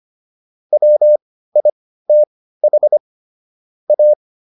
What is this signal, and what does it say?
Morse code, heard as a keyed tone, about 25 wpm, spaced as for 15 wpm, 600 Hz. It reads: WITH A
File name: SS203 15